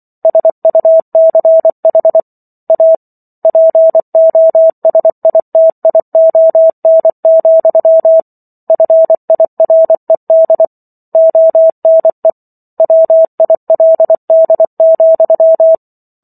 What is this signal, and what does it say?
Morse code, heard as a keyed tone, about 24 wpm, 650 Hz. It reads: SUCH A POSITION, FIRED ONE WILD,